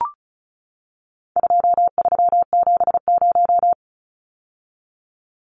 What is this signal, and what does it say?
Morse code, heard as a keyed tone, about 34 wpm, 700 Hz. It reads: 2370
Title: Numbers - 4-Digit 35wpm